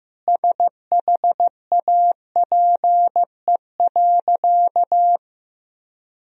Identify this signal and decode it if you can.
Morse code, heard as a keyed tone, about 16 wpm, 700 Hz. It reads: SHAPE.
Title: Sentences from Top 1000 Words - Review Part 3 - 15wpm